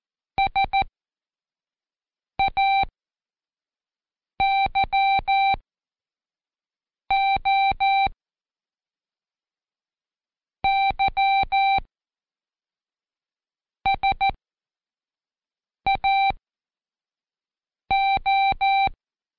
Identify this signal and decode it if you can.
Morse code, heard as a keyed tone, about 14 wpm, spaced as for 5 wpm, 780 Hz. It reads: SAYO YSAO
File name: Code Quick / AudioDownld1 / Seg18-Day2-Quiz ASYO